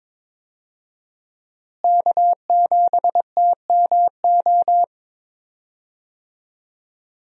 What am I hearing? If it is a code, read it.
Morse code, heard as a keyed tone, about 21 wpm, 700 Hz. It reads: K7TMO
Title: USA Call Signs - 5 Characters - Repeated 2x - 22wpm